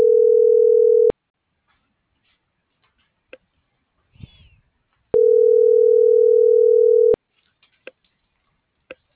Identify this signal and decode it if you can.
Ringback tone (North America), 440 + 480 Hz.